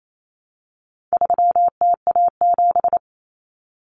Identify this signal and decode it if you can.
Morse code, heard as a keyed tone, about 28 wpm, 700 Hz. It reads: 3TA7
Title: Common Cut Numbers - 4-Digit 28wpm